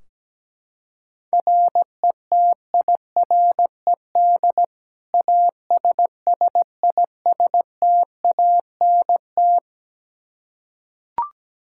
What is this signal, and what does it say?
Morse code, heard as a keyed tone, about 17 wpm, 700 Hz. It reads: RETIRED ASSISTANT
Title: Practice QSO - Exchange 3 - More Info - 17wpm